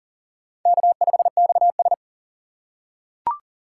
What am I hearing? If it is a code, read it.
Morse code, heard as a keyed tone, about 40 wpm, 700 Hz. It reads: K5XS
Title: USA Call Signs - 4 Characters - 4x Character Spacing - 40wpm